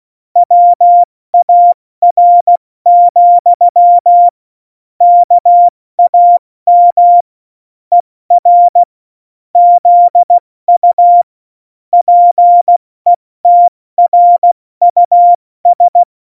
Morse code, heard as a keyed tone, about 16 wpm, 700 Hz. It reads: WAR, KAM ER ZU PETRUS
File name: Morsekurs / Witze / Witz 26. Januar 2021, 70 F80